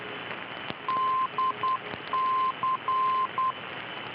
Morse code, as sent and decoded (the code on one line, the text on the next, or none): -.. -.-.
DC